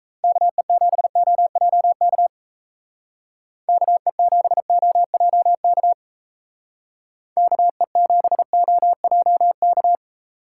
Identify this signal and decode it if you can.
Morse code, heard as a keyed tone, about 33 wpm, 700 Hz. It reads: KE7OJK KE7OJK KE7OJK
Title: Speed Racing - USA 2x3 Call Signs 28wpm